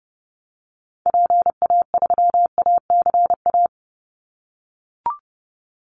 Morse code, heard as a keyed tone, about 30 wpm, 700 Hz. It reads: PA3ACA
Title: Call Signs - 30wpm